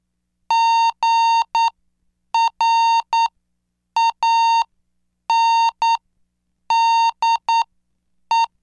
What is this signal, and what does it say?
Morse code, heard as a keyed tone, about 9 wpm, spaced as for 7 wpm, 920 Hz. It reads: GRANDE